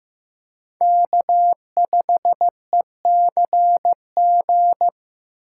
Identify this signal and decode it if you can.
Morse code, heard as a keyed tone, about 15 wpm, 700 Hz. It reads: K5ECG
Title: USA Call Signs - 5 Characters - 3x Character Spacing - 15wpm